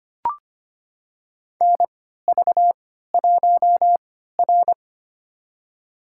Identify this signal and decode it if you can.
Morse code, heard as a keyed tone, about 25 wpm, spaced as for 14 wpm, 700 Hz. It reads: NV1R
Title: USA Call Signs - 4 Characters - 3x Character Spacing - 25wpm